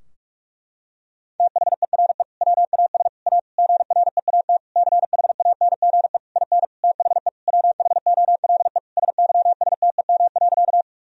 Morse code, heard as a keyed tone, about 41 wpm, 700 Hz. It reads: THERE WAS A GREAT CHANGE IN THE WHOLE SYSTEM.
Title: Sentences from Top 1000 Words - Review Part 3 - 45wpm